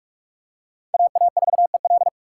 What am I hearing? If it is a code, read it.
Morse code, heard as a keyed tone, about 46 wpm, 700 Hz. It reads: AA4EL
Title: USA Call Signs - 5 Characters - 2x Character Spacing - 45wpm